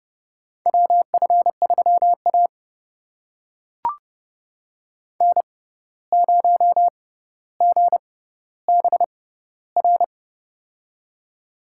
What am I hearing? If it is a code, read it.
Morse code, heard as a keyed tone, about 30 wpm, spaced as for 14 wpm, 700 Hz. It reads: WF3A N0GBR
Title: USA Call Signs - 6x Character Spacing - 30wpm